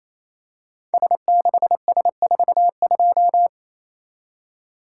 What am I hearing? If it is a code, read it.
Morse code, heard as a keyed tone, about 27 wpm, 700 Hz. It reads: S6S42
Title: Code Groups - 5 Letter:Numbers - 28wpm